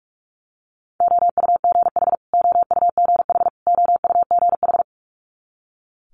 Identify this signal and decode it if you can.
Morse code, heard as a keyed tone, about 46 wpm, 700 Hz. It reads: OUGH OUGH OUGH
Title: Multi-letter Phonemes Repeated 3x 45wpm